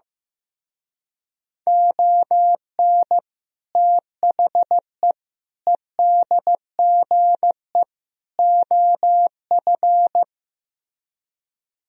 Morse code, heard as a keyed tone, about 16 wpm, 700 Hz. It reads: ON THE EDGE OF
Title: Speed-Racing - Sets of 4 Words 15wpm